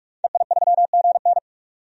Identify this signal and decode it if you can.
Morse code, heard as a keyed tone, about 46 wpm, 700 Hz. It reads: EI3GN
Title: DXCC-100 Call Signs - 5 Characters - 45wpm